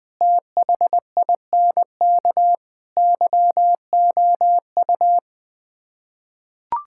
Morse code, heard as a keyed tone, about 21 wpm, 700 Hz. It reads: THINK YOU
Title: Build A Phrase - 5-Word Phrases - 20wpm